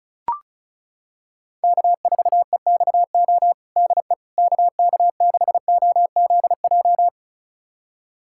Morse code, heard as a keyed tone, about 34 wpm, 700 Hz. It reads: K4EXO DE KK6OZJ
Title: USA Callsign DE Callsign 35wpm